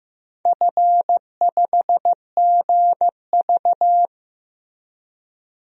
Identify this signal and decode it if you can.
Morse code, heard as a keyed tone, about 15 wpm, 700 Hz. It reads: F5GV
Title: DXCC-100 Call Signs - 4 Characters - 15wpm